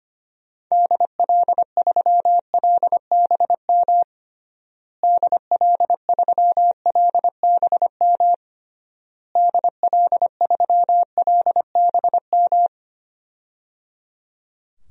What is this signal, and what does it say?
Morse code, heard as a keyed tone, about 25 wpm, 700 Hz. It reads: DL3LBM DL3LBM DL3LBM
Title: DXCC-100 Call Signs - 6 Characters - Repeated 3x - 25wpm